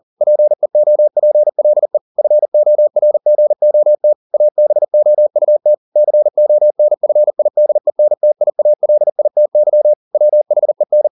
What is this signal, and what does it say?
Morse code, heard as a keyed tone, about 40 wpm, 600 Hz. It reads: PEOPLE FORGOT ABOUT CONFIDENTIALITY WHEN